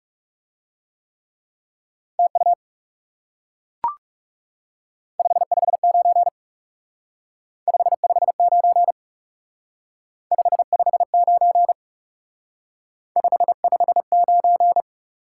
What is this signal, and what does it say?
Morse code, heard as a keyed tone, about 38 wpm, 700 Hz. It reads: TU 559 559 559 559